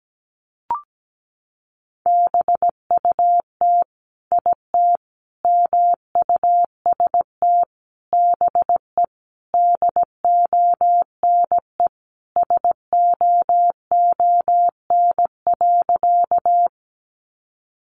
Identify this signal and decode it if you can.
Morse code, heard as a keyed tone, about 17 wpm, 700 Hz. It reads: BUT IT MUST BE DONE SOON.
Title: Sentences from Top 500 Words 17wpm